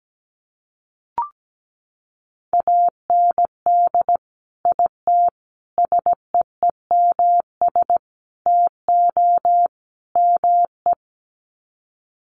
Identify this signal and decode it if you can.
Morse code, heard as a keyed tone, about 17 wpm, 700 Hz. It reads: AND IT SEEMS TO ME
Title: Sets of 5 Words Encore2 17wpm